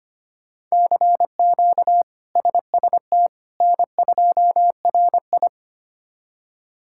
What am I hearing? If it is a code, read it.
Morse code, heard as a keyed tone, about 25 wpm, 700 Hz. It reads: CQ SST N2RI
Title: K1USN SST QSO Practice - 25wpm